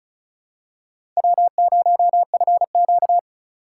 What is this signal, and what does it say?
Morse code, heard as a keyed tone, about 34 wpm, 700 Hz. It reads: W0FQ
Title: USA Call Signs - 4 Characters - 3x Character Spacing - 35wpm